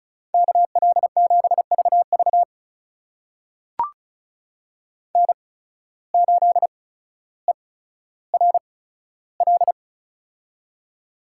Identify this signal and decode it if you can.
Morse code, heard as a keyed tone, about 36 wpm, spaced as for 15 wpm, 700 Hz. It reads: KL7VV N8ERL
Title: USA Call Signs - 5 Characters - 8x Character Spacing - 35wpm